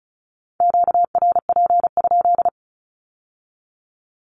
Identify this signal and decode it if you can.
Morse code, heard as a keyed tone, about 36 wpm, 700 Hz. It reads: QRP?
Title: Q-Codes with Spoken Meaning - 35wpm